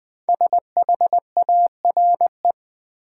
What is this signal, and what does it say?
Morse code, heard as a keyed tone, about 20 wpm, 700 Hz. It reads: SHARE
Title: Sentences from Top 900 Words - New Word Drill - 20wpm